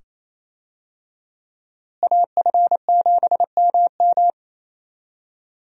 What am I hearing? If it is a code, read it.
Morse code, heard as a keyed tone, about 28 wpm, 700 Hz. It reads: AF7MM
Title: USA Call Signs - 5 Characters - Repeated 2x - 28wpm